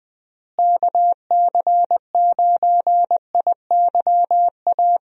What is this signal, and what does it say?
Morse code, heard as a keyed tone, about 20 wpm, 700 Hz. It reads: KC9IYA
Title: USA Call Signs - 6 Characters - Repeated 3x - 20wpm